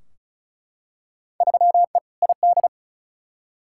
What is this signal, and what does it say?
Morse code, heard as a keyed tone, about 36 wpm, 700 Hz. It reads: 3E ID
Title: Field Day Exchanges Encore - 35wpm